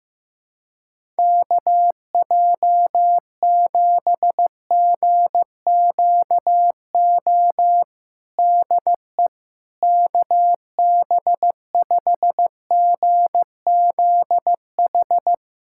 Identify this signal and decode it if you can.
Morse code, heard as a keyed tone, about 15 wpm, 700 Hz. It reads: KJ7GQO DE KB5GZH
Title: USA Callsign DE Callsign - 6 Characters - 15wpm